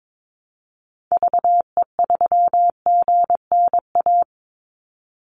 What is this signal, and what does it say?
Morse code, heard as a keyed tone, about 22 wpm, 700 Hz. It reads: VE3GNA
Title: Call Signs - 22wpm